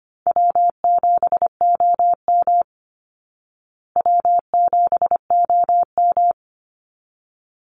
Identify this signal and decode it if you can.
Morse code, heard as a keyed tone, about 25 wpm, 700 Hz. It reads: W7OM W7OM
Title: Easy Call Signs Repeated 2x - 25wpm